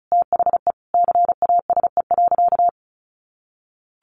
Morse code, heard as a keyed tone, about 34 wpm, 700 Hz. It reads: THE CASE.
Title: Sentences from Top 1300 Words 35wpm